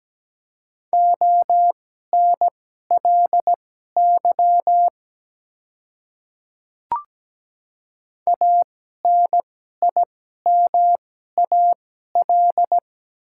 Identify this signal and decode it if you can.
Morse code, heard as a keyed tone, about 17 wpm, spaced as for 12 wpm, 700 Hz. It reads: ONLY ANIMAL
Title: 2x Character Spacing - Top 200 Words - 17wpm